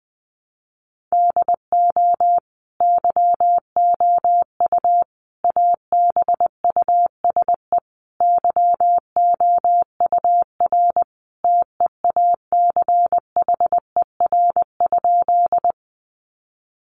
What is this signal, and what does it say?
Morse code, heard as a keyed tone, about 20 wpm, 700 Hz. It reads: DO YOU ABUSE YOUR TEACHER?
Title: Sentences from Top 1300 Words 20wpm